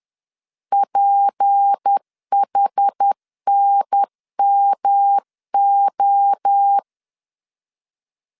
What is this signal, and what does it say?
Morse code, heard as a keyed tone, about 10 wpm, 800 Hz. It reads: PHNMO
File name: LEZIONI / 6 PKX0 / 6 P 10wpm 2432